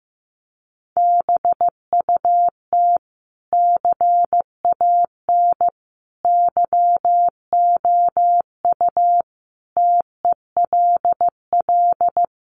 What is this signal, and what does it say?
Morse code, heard as a keyed tone, about 15 wpm, 700 Hz. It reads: BUT CAN YOU TELL
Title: Sentences from Top 1400 Words 15wpm